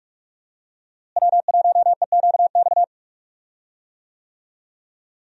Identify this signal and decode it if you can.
Morse code, heard as a keyed tone, about 44 wpm, 700 Hz. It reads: W1EQX
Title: USA Call Signs - 5 Characters - Repeated 3x - 45wpm